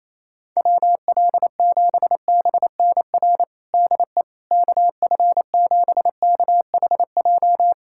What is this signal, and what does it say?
Morse code, heard as a keyed tone, about 28 wpm, 700 Hz. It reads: WL7BNR DE KF7KHJ
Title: USA Callsign DE Callsign - 6 Characters - 28wpm